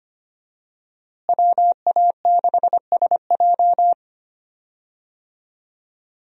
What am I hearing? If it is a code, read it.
Morse code, heard as a keyed tone, about 25 wpm, 700 Hz. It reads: WA6SJ
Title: USA Call Signs - 5 Characters - Repeated 2x - 25wpm